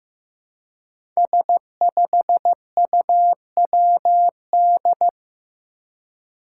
Morse code, heard as a keyed tone, about 15 wpm, 700 Hz. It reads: S5UWD
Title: Five Challenging Letter-Number Combinations 15wpm